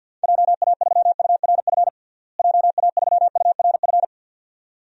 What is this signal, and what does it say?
Morse code, heard as a keyed tone, about 52 wpm, 700 Hz. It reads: JA3URF JA3URF
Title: DXCC-100 Call Signs - 6 Characters - Repeated 2x - 50wpm